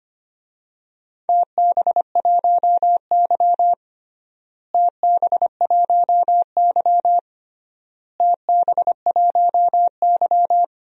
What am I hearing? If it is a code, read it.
Morse code, heard as a keyed tone, about 25 wpm, 700 Hz. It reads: TB1Y TB1Y TB1Y
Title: DXCC-100 Call Signs - 4 Characters - Repeated 3x - 25wpm